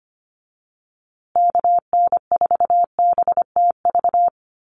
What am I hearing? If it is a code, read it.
Morse code, heard as a keyed tone, about 25 wpm, 700 Hz. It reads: KN4BTV
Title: USA Call Signs Repeated 2x 25wpm